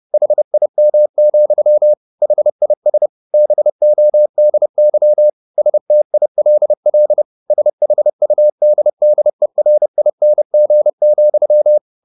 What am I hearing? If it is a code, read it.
Morse code, heard as a keyed tone, about 30 wpm, 600 Hz. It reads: HIM, HIS BODY STILL SHUDDERING,
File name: the old man and the sea / the old man and the sea 1 30wpm